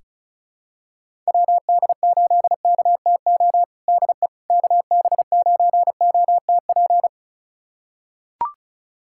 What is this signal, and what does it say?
Morse code, heard as a keyed tone, about 36 wpm, 700 Hz. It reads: WD8KTO DE KB9OTP
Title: USA Callsign DE Callsign - 6 Characters - 35wpm